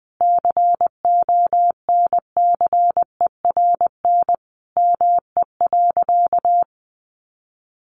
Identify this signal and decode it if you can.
Morse code, heard as a keyed tone, about 20 wpm, 700 Hz. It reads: CONCERN ME.
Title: Sentences from Top 1000 Words 20wpm